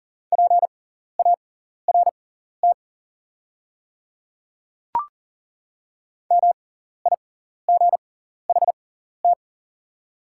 Morse code, heard as a keyed tone, about 40 wpm, spaced as for 14 wpm, 700 Hz. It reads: PART MIGHT
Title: 6x Character Spacing - Top 300 Words - 40wpm